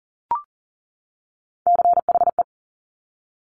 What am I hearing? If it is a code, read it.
Morse code, heard as a keyed tone, about 42 wpm, 690 Hz. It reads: CHE
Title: Top Three Letter Combinations in Words 40wpm